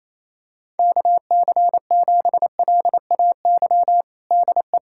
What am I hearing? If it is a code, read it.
Morse code, heard as a keyed tone, about 28 wpm, 700 Hz. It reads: KC7LAY DE
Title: USA Callsign DE Callsign - 6 Characters - 28wpm